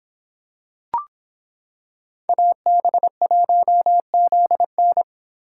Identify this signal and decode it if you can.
Morse code, heard as a keyed tone, about 26 wpm, 700 Hz. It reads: AB1ZN